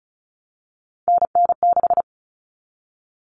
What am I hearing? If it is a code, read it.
Morse code, heard as a keyed tone, about 34 wpm, 700 Hz. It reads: NN6
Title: Common Cut Numbers - 3-Digit 35wpm